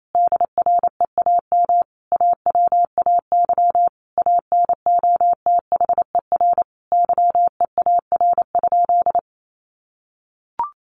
Morse code, heard as a keyed tone, about 28 wpm, 700 Hz. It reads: DREAM AWAY ANOTHER YEAR?
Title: Sentences from Top 1500 Words 28wpm